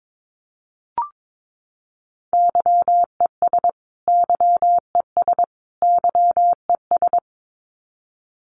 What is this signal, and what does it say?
Morse code, heard as a keyed tone, about 22 wpm, 700 Hz. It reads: YES YES YES
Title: Top 500 Words Repeated 22wpm